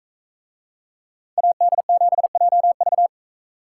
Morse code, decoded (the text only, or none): AD7JV